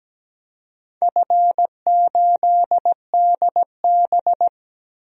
Morse code, heard as a keyed tone, about 17 wpm, 700 Hz. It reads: F8DB
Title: DXCC-100 Call Signs - 4 Characters - 17wpm